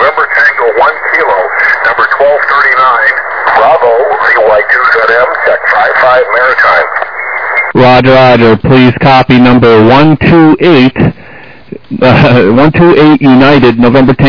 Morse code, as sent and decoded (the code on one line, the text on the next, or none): none